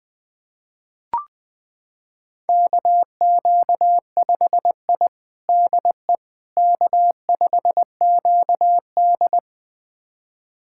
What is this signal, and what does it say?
Morse code, heard as a keyed tone, about 20 wpm, 700 Hz. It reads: KQ5I DE K5QD
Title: USA Callsign DE Callsign - 4 Characters - 20wpm